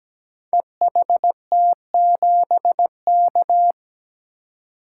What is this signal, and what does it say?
Morse code, heard as a keyed tone, about 17 wpm, 700 Hz. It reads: EHT7K
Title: Code Groups - 5 Letter:Numbers - 17wpm